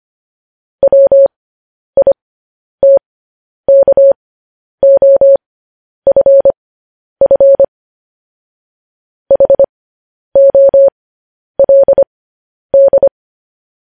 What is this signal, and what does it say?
Morse code, heard as a keyed tone, about 25 wpm, spaced as for 10 wpm, 550 Hz. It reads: WITKOFF HOLD
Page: News Headlines in Morse Code at 10 WPM